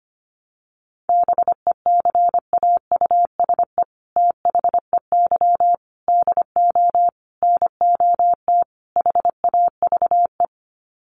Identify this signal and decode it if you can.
Morse code, heard as a keyed tone, about 25 wpm, 700 Hz. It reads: BECAUSE THEY DO NOT HAVE
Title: Sets of 5 Words Encore2 25wpm